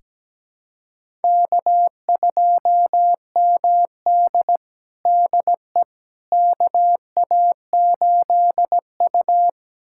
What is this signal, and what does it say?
Morse code, heard as a keyed tone, about 17 wpm, 700 Hz. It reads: K2MD DE KA8U
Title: USA Callsign DE Callsign - 4 Characters - 17wpm